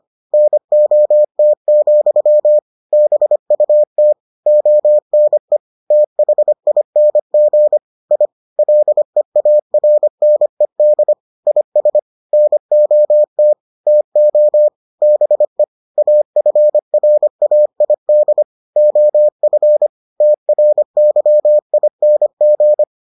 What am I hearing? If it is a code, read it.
Morse code, heard as a keyed tone, about 25 wpm, 600 Hz. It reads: NOT, BUT ONE THING I LEARNED IS NOT TO BE AFRAID OF TRYING